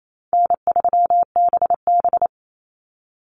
Morse code, heard as a keyed tone, about 28 wpm, 700 Hz. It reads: N3BB
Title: Easy Call Signs Repeated 2x - 28wpm